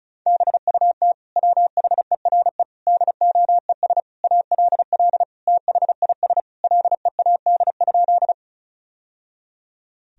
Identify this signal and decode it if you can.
Morse code, heard as a keyed tone, about 34 wpm, 700 Hz. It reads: BUT WHERE DOES ALL THIS LEAD?
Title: Sentences from Top 1000 Words - Review Part 2 - 35wpm